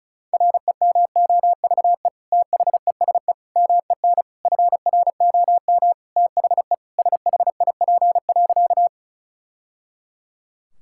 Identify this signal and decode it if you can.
Morse code, heard as a keyed tone, about 34 wpm, 700 Hz. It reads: REMOVE THESE MEN FROM THE SHIP.
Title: Sentences from Top 1000 Words - Review Part 5 - 35wpm